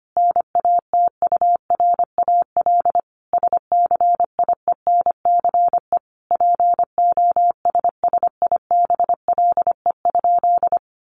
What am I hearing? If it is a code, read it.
Morse code, heard as a keyed tone, about 25 wpm, 700 Hz. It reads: NATURAL SCIENCE POSSIBLE?
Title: Sentences from Top 1500 Words 25wpm